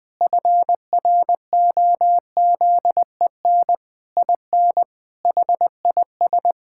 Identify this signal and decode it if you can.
Morse code, heard as a keyed tone, about 20 wpm, 700 Hz. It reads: FROZEN IN HIS
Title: Pangram Challenge v2 - 20wpm